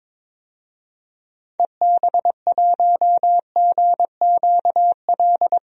Morse code, heard as a keyed tone, about 22 wpm, 700 Hz. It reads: EB1GQL
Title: DXCC-100 Call Signs - 6 Characters - 22wpm